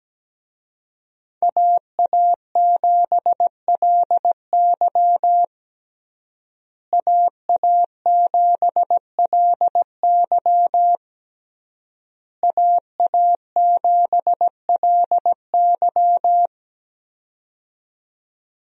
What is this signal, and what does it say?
Morse code, heard as a keyed tone, about 17 wpm, 700 Hz. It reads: AA7LY AA7LY AA7LY
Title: USA Call Signs - 5 Characters - Repeated 3x - 17wpm